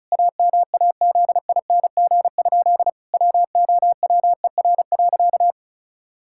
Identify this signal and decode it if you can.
Morse code, heard as a keyed tone, about 36 wpm, 700 Hz. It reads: AMAZING? WOWER.
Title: Pangram Challenge v2 - 35wpm